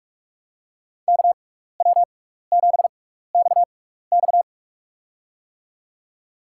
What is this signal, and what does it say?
Morse code, heard as a keyed tone, about 46 wpm, spaced as for 15 wpm, 690 Hz. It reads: KW7XX
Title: USA Call Signs - 5 Characters - 6x Character Spacing - 45wpm